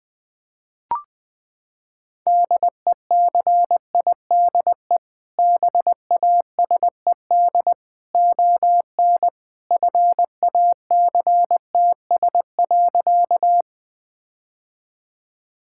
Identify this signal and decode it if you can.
Morse code, heard as a keyed tone, about 21 wpm, 700 Hz. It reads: DECIDE BASED ON FACTS.
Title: Sentences with 2-Word Phrases - 20wpm